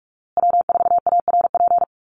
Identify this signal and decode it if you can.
Morse code, heard as a keyed tone, about 44 wpm, 700 Hz. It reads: W4ARP
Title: USA Call Signs Repeated 3x 45wpm